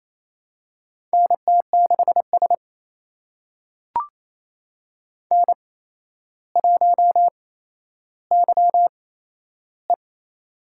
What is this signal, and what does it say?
Morse code, heard as a keyed tone, about 27 wpm, spaced as for 12 wpm, 700 Hz. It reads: NT6S N1YE
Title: USA Call Signs - 4 Characters - 8x Character Spacing - 28wpm